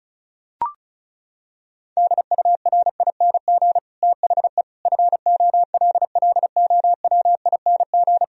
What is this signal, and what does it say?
Morse code, heard as a keyed tone, about 36 wpm, 700 Hz. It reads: DURING THE FOLLOWING
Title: Sentences from Top 1000 Words - Review Part 4 - 35wpm